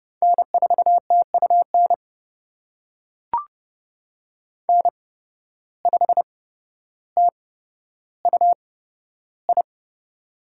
Morse code, heard as a keyed tone, about 32 wpm, spaced as for 13 wpm, 700 Hz. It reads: N4TUN N5TUI
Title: USA Call Signs - 5 Characters - 8x Character Spacing - 30wpm